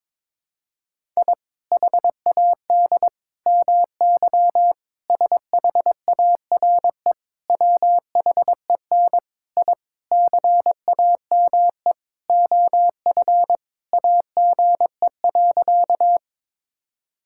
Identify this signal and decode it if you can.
Morse code, heard as a keyed tone, about 22 wpm, 700 Hz. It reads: I HAD MY SHARE WHEN I CAME OF AGE.
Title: Sentences from Top 1000 Words - Review Part 6 - 22wpm